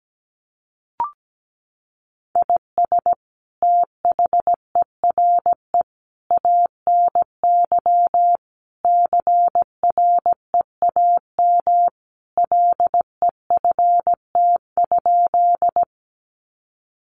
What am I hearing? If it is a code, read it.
Morse code, heard as a keyed tone, about 17 wpm, 700 Hz. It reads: IS THERE ANY CREAM LEFT?
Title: Sentences from Top 1500 Words 17wpm